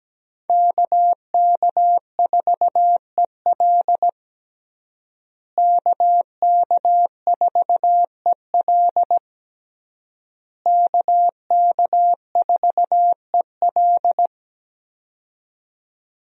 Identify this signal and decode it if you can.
Morse code, heard as a keyed tone, about 17 wpm, 700 Hz. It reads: KK4EL KK4EL KK4EL
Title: USA Call Signs - 5 Characters - Repeated 3x - 17wpm